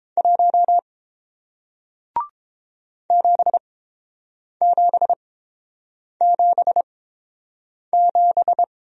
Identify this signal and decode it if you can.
Morse code, heard as a keyed tone, about 31 wpm, spaced as for 7 wpm, 700 Hz. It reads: 1 7777